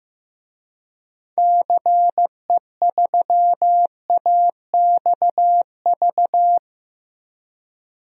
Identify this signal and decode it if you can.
Morse code, heard as a keyed tone, about 15 wpm, 700 Hz. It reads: CE3AXV
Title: DXCC-100 Call Signs - 6 Characters - Repeated 2x - 15wpm